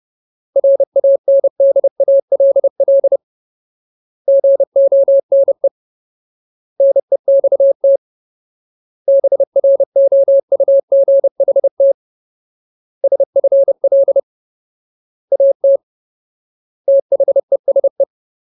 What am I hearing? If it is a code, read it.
Morse code, heard as a keyed tone, about 30 wpm, 550 Hz. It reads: RANDALL GONE NEXT BROUGHT SFL AT THESE
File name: Ultimate Word List / 30WPM Ultimate